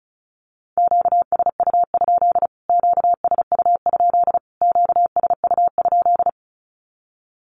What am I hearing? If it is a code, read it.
Morse code, heard as a keyed tone, about 34 wpm, 700 Hz. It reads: QSU? QSU? QSU?
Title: Q-Codes Repeated 3x 35wpm